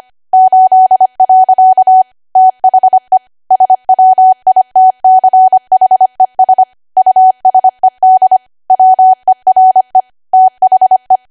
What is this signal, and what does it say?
Morse code, heard as a keyed tone, about 25 wpm, 750 Hz. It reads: 8. THE SWITCHES USED WERE THE
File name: cwa / W1AW-Level3 / 25WPM